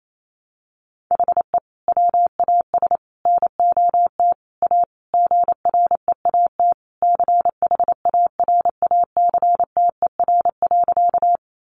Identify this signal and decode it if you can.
Morse code, heard as a keyed tone, about 28 wpm, 700 Hz. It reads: HE WAS NOT A GREAT CHARACTER.
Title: Sentences from Top 1200 Words 28wpm